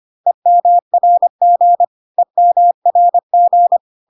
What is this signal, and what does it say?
Morse code, heard as a keyed tone, about 25 wpm, 700 Hz. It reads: EMRG EMRG
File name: CW abbreviations alphabetical repeated 3 times 25 25wpm